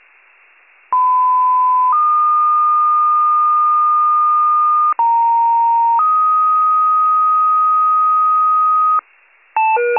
Two-tone page, heard as a one-second tone, then a three-second tone, about 1010 then 1220 Hz, about 930 then 1250 Hz. It is heard from 0.9 to 9.0 s.